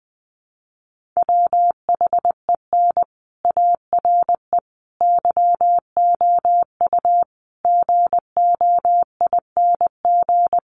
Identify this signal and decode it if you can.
Morse code, heard as a keyed tone, about 20 wpm, 700 Hz. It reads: WHEN ARE YOU GOING
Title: Sets of 5 Words Encore2 20wpm